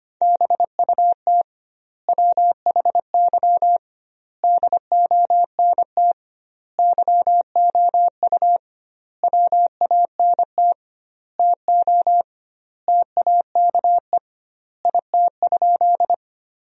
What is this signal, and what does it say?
Morse code, heard as a keyed tone, about 25 wpm, 700 Hz. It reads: BUT WHY DONT YOU WANT TO TAKE IT?
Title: 2x Word-Spacing - Sentences from Top 300 Words - 25wpm